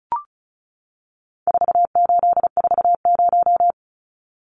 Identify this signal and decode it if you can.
Morse code, heard as a keyed tone, about 34 wpm, 700 Hz. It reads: 4840
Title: Numbers - 4-Digit 35wpm